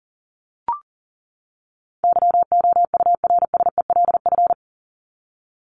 Morse code, heard as a keyed tone, about 40 wpm, 700 Hz. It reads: YOURSELF
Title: Top 501-750 Words 40wpm